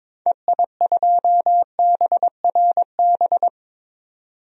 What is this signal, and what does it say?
Morse code, heard as a keyed tone, about 21 wpm, 700 Hz. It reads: EI2BRB
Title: DXCC-100 Call Signs - 6 Characters - 22wpm